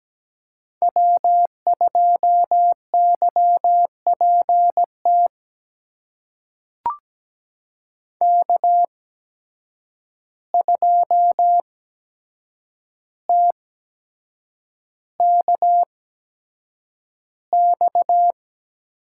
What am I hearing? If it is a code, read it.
Morse code, heard as a keyed tone, about 17 wpm, spaced as for 7 wpm, 700 Hz. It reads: W2YPT K2TKX
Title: USA Call Signs - 5 Characters - 8x Character Spacing - 17wpm